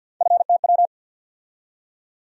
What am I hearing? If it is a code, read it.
Morse code, heard as a keyed tone, about 53 wpm, 700 Hz. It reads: FTW
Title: USA Call Sign Postfixes - 3 Letters 50wpm